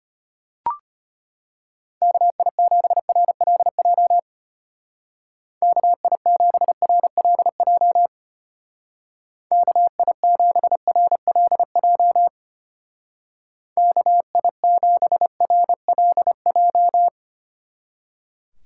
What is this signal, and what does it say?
Morse code, heard as a keyed tone, about 32 wpm, 700 Hz. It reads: KI7RLJ KI7RLJ KI7RLJ KI7RLJ